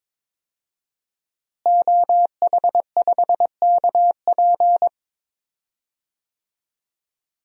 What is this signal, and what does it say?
Morse code, heard as a keyed tone, about 21 wpm, 700 Hz. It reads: OH5KP
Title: DXCC-100 Call Signs - 5 Characters - Repeated 2x - 22wpm